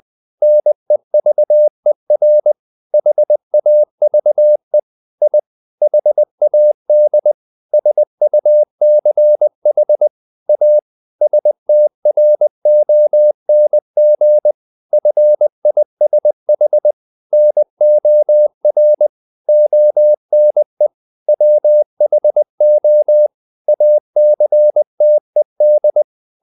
Morse code, heard as a keyed tone, about 20 wpm, 600 Hz. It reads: NEVER HAVE I HAD SUCH A STRONG FISH NOR ONE WHO ACTED
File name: the old man and the sea / the old man and the sea part 1 20wpm